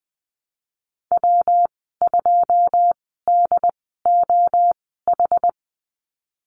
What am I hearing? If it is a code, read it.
Morse code, heard as a keyed tone, about 20 wpm, spaced as for 15 wpm, 700 Hz. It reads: W2DOH
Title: 213 Lesson 34g - Number 8 - Call Signs 20wpm